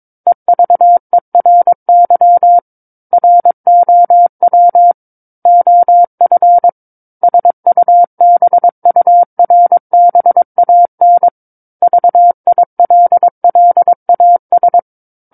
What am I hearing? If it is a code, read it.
Morse code, heard as a keyed tone, about 22 wpm, 700 Hz. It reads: EVERY ROW OF SUBURBAN VILLAS
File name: Book One / 13-Chapter 13 - 22 WPM